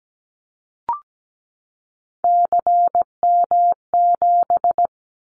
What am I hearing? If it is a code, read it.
Morse code, heard as a keyed tone, about 17 wpm, 700 Hz. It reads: CM7
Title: Callsign Prefixes for Top 100 DXCC - 17wpm